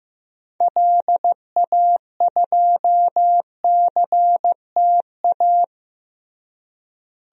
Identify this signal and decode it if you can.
Morse code, heard as a keyed tone, about 16 wpm, 700 Hz. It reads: LA2CTA
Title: DXCC-100 Call Signs - 6 Characters - Repeated 2x - 15wpm